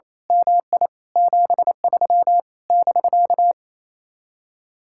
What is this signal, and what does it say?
Morse code, heard as a keyed tone, about 26 wpm, 700 Hz. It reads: MI 73 <BK>
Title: Speed-Racing Common QSO Phrases Encore - 28wpm